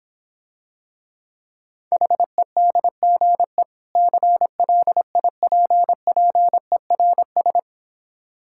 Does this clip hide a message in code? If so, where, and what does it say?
Yes: Morse code from 1.9 to 7.6 s, heard as a keyed tone, about 26 wpm, 700 Hz. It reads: HEDGE CLIPPERS